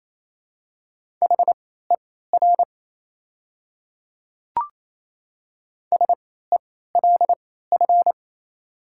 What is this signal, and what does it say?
Morse code, heard as a keyed tone, about 28 wpm, spaced as for 16 wpm, 700 Hz. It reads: HER SELF
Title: 3x Character Spacing - Top 200 Words - 28wpm